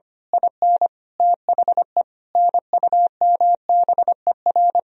Morse code, heard as a keyed tone, about 25 wpm, 700 Hz. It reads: IN THE NUMBER
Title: Speed-Racing - Sets of 5 Words 25wpm